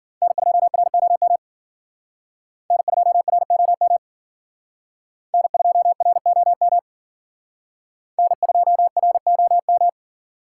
Morse code, heard as a keyed tone, about 51 wpm, 700 Hz. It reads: N2ROM N2ROM N2ROM N2ROM